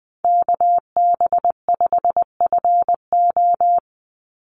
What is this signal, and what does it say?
Morse code, heard as a keyed tone, about 19 wpm, 700 Hz. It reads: KB5FO
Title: USA Call Signs 20wpm